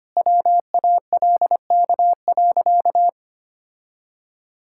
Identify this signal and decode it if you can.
Morse code, heard as a keyed tone, about 25 wpm, 700 Hz. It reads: WALK.
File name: Easy Sentences with Hints - 25wpm